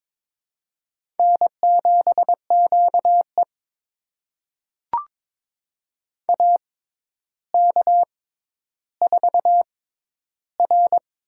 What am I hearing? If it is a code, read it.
Morse code, heard as a keyed tone, about 22 wpm, spaced as for 11 wpm, 700 Hz. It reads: N7QE AK4R
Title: USA Call Signs - 4 Characters - 6x Character Spacing - 22wpm